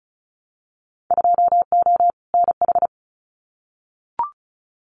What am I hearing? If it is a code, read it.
Morse code, heard as a keyed tone, about 36 wpm, 700 Hz. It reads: 2O NH
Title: Winter Field Day Exchanges Encore 35wpm